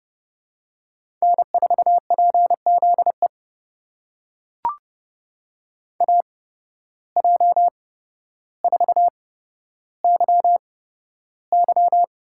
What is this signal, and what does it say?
Morse code, heard as a keyed tone, about 30 wpm, spaced as for 13 wpm, 700 Hz. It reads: N4PZE AJ4YY
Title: USA Call Signs - 8x Character Spacing - 30wpm